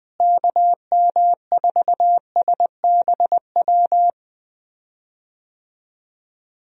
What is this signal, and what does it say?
Morse code, heard as a keyed tone, about 20 wpm, 700 Hz. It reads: KM4SBW
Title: USA Call Signs - 6 Characters - Repeated 3x - 20wpm